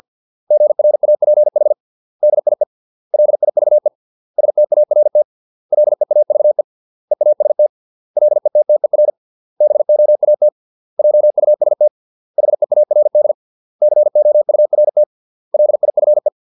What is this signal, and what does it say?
Morse code, heard as a keyed tone, about 50 wpm, 600 Hz. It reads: GRAPH DIE LIVE START LEAVE EAST LETTER BOAT JUST HEARD COURT LIFE